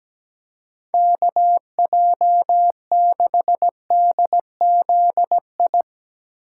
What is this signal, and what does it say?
Morse code, heard as a keyed tone, about 18 wpm, 700 Hz. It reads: KJ6DZI
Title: Speed Racing - USA Call Signs 17wpm